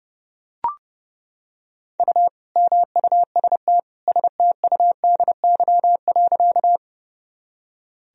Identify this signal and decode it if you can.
Morse code, heard as a keyed tone, about 30 wpm, 700 Hz. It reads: U MUST STUDY.
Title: Sentences from Top 300 Words and Abbreviations 30wpm